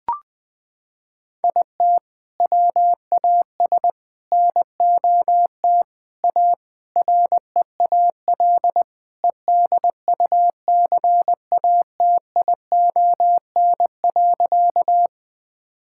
Morse code, heard as a keyed tone, about 20 wpm, 700 Hz. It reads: IT WAS NOT A REAL EDUCATION.
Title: Sentences from Top 1000 Words - Review Part 6 - 20wpm